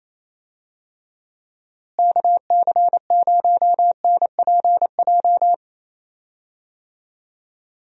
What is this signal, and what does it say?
Morse code, heard as a keyed tone, about 28 wpm, 700 Hz. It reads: KC0NPJ
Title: USA Call Signs - 6 Characters - Repeated 3x - 28wpm